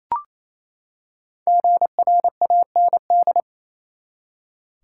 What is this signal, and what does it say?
Morse code, heard as a keyed tone, about 28 wpm, 700 Hz. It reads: GRAND
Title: Five Letter Words - 28wpm